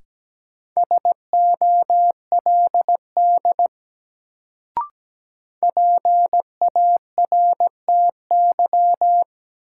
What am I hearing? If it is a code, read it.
Morse code, heard as a keyed tone, about 17 wpm, 700 Hz. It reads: SOLD PARTY
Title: Sentences from Top 1400 Words - New Word Drill Alt - 17wpm